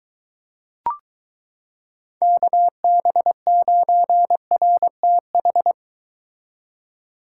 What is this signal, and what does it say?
Morse code, heard as a keyed tone, about 23 wpm, 700 Hz. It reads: KB9RTH